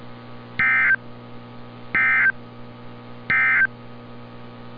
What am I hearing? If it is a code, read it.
Emergency Alert System (SAME) tones, heard as harsh buzzing bursts. It is heard from 0.5 to 3.8 s.
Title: NOAA Weather Radio - Altoona, PA